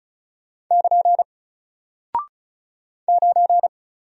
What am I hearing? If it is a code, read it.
Morse code, heard as a keyed tone, about 35 wpm, 700 Hz. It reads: <KN> 9
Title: Single Character (Letter:Number:Punctuation:Prosign) Alt 35wpm